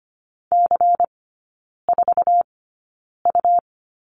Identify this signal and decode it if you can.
Morse code, heard as a keyed tone, about 25 wpm, spaced as for 9 wpm, 700 Hz. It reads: C4U
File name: Lesson 17e - Letter u - Three Characters 25wpm